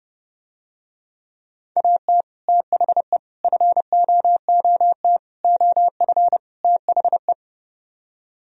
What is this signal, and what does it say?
Morse code, heard as a keyed tone, about 29 wpm, 700 Hz. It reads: AT THE FOOT OF THE